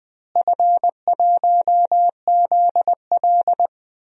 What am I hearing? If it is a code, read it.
Morse code, heard as a keyed tone, about 20 wpm, 700 Hz. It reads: F1ZL
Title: DXCC-100 Call Signs - 4 Characters - Repeated 2x - 20wpm